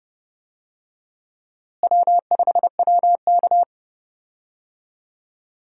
Morse code, heard as a keyed tone, about 30 wpm, 700 Hz. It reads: W5WK
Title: USA Call Signs - 4 Characters - Repeated 2x - 30wpm